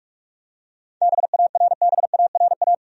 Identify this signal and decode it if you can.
Morse code, heard as a keyed tone, about 42 wpm, 700 Hz. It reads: BARBARA
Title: USA Common Names - 45wpm